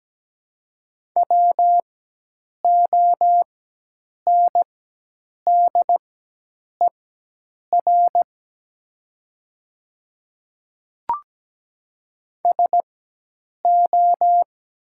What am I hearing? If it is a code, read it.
Morse code, heard as a keyed tone, about 17 wpm, spaced as for 8 wpm, 700 Hz. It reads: WONDER SO
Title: 4x Character Spacing - Top 500 Words - 17wpm